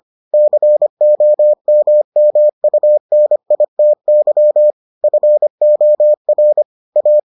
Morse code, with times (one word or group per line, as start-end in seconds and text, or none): 0.3-4.7 COMMUNITY
5.0-6.6 FOR
7.0-7.2 A